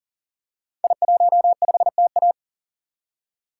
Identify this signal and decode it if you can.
Morse code, heard as a keyed tone, about 40 wpm, 700 Hz. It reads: I15TA
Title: Code Groups - 5 Letter:Numbers - 40wpm